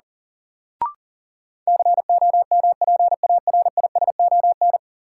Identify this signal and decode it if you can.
Morse code, heard as a keyed tone, about 37 wpm, 700 Hz. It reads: COMPARISON
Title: Sentences from Top 1600 Words - New Word Drill - 40wpm